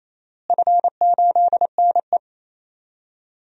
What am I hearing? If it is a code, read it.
Morse code, heard as a keyed tone, about 28 wpm, 700 Hz. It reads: F8NE
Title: DXCC-100 Call Signs - 4 Characters - Repeated 2x - 28wpm